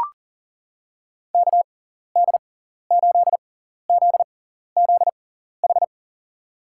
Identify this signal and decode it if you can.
Morse code, heard as a keyed tone, about 38 wpm, spaced as for 14 wpm, 700 Hz. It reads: KD8ZZH
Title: USA Call Signs - 6 Characters - 6x Character Spacing - 40wpm